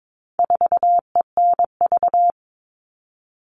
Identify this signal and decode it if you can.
Morse code, heard as a keyed tone, about 22 wpm, 700 Hz. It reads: 4ENV
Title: All Cut Numbers - 4-Digit 22wpm